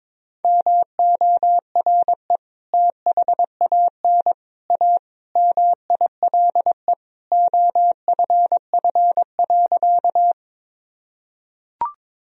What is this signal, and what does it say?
Morse code, heard as a keyed tone, about 21 wpm, 700 Hz. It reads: MORE THAN A MILE OFF.
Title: Sentences from Top 1000 Words - Review Part 1 - 22wpm